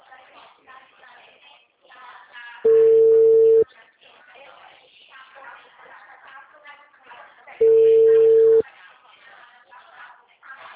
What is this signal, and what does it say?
Ringback tone (China, 450 Hz), with speech.
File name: out- -2020- - - .20